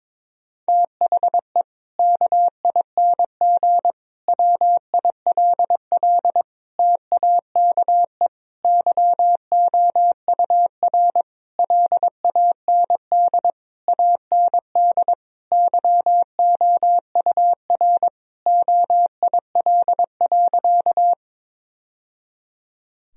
Morse code, heard as a keyed tone, about 22 wpm, 700 Hz. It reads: THE KING WILL TAKE YOUR LAND AND YOUR OIL.
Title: Sentences from Top 1000 Words - Review Part 4 - 22wpm